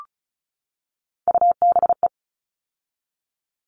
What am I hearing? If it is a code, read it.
Morse code, heard as a keyed tone, about 34 wpm, 700 Hz. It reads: UBE
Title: All Cut Numbers - 3-Digit 35wpm